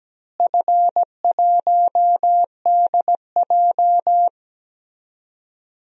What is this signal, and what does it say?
Morse code, heard as a keyed tone, about 17 wpm, 700 Hz. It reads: F1DJ
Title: DXCC-100 Call Signs - 4 Characters - Repeated 3x - 17wpm